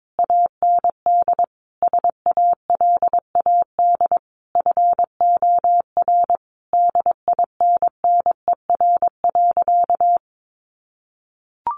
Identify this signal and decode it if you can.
Morse code, heard as a keyed tone, about 22 wpm, 700 Hz. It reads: AND SALAD FOR DINNER.
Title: Easy Phrases 22wpm